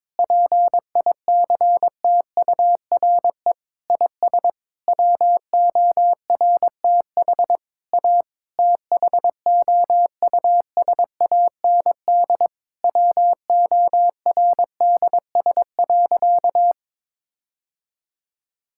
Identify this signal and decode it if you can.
Morse code, heard as a keyed tone, about 22 wpm, 700 Hz. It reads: PICTURE IS WORTH A THOUSAND WORDS.
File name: Sentences with 7-Word Phrases - 22wpm